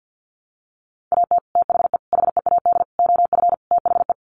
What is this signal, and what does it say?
Morse code, heard as a keyed tone, about 48 wpm, 700 Hz. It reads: AT THE HEAD OF THE
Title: Sets of 5 Words 50wpm